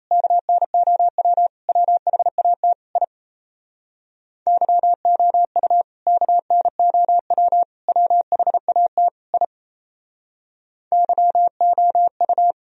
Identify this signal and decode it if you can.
Morse code, heard as a keyed tone, about 37 wpm, 700 Hz. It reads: KNOW WHAT I YOU KNOW WHAT I YOU